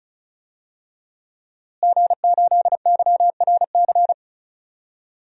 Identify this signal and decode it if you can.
Morse code, heard as a keyed tone, about 36 wpm, 700 Hz. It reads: G8YRC
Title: DXCC-100 Call Signs - 5 Characters - Repeated 2x - 35wpm